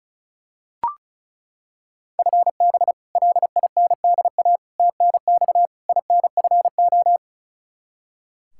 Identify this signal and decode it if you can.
Morse code, heard as a keyed tone, about 36 wpm, 700 Hz. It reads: FB LINDA TNX INFO
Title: Practice QSO - Exchange 3 - More Info - 35wpm